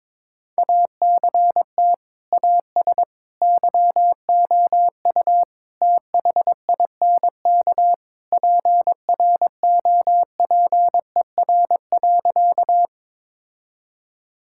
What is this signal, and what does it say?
Morse code, heard as a keyed tone, about 22 wpm, 700 Hz. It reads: ACT AS YOU THINK PROPER.
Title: Sentences from Top 1000 Words - Review Part 5 - 22wpm